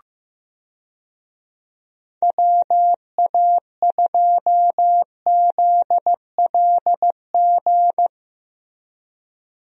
Morse code, heard as a keyed tone, about 15 wpm, 700 Hz. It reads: WA2ZLG
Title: USA Call Signs - 6 Characters - Repeated 3x - 15wpm